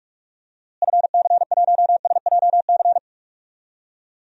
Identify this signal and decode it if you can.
Morse code, heard as a keyed tone, about 46 wpm, 700 Hz. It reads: FC1SJC
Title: DXCC-100 Call Signs - 6 Characters - Repeated 3x - 45wpm